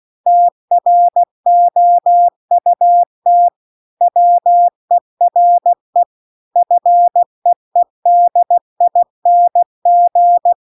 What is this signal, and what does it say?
Morse code, heard as a keyed tone, about 16 wpm, 700 Hz. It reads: TROUT WERE FEEDING